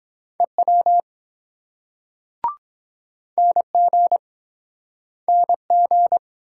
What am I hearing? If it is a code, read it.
Morse code, heard as a keyed tone, about 23 wpm, 700 Hz. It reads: EW NG NG